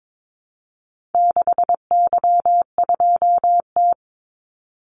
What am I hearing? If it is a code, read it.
Morse code, heard as a keyed tone, about 22 wpm, 700 Hz. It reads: 6Y2T
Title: Call Signs Repeated 3x - 22wpm